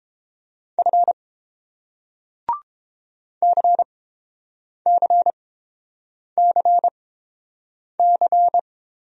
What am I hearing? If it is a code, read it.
Morse code, heard as a keyed tone, about 30 wpm, spaced as for 7 wpm, 700 Hz. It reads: F CCCC